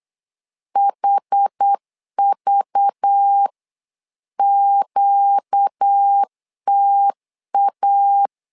Morse code, heard as a keyed tone, about 8 wpm, 800 Hz. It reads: HVQTA
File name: LEZIONI / 5 LBVC / 5 LBV 8wpm 2837